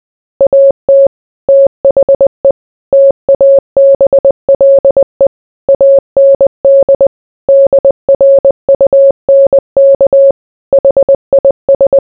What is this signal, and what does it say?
Morse code, heard as a keyed tone, about 20 wpm, 550 Hz. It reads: AT THE TABLE AND DRUNK HIS